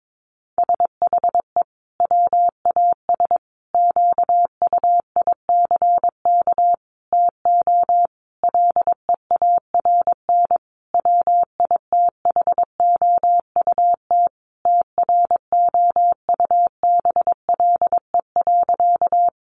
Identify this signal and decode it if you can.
Morse code, heard as a keyed tone, about 22 wpm, 700 Hz. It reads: SHE WAS QUICK TO LEARN WITHOUT TROUBLE.
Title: Sentences from Top 1200 Words 22wpm